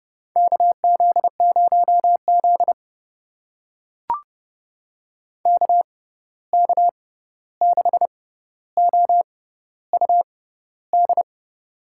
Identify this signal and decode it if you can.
Morse code, heard as a keyed tone, about 30 wpm, spaced as for 14 wpm, 700 Hz. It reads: KZ0Z KK6OUD
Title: USA Call Signs - 6x Character Spacing - 30wpm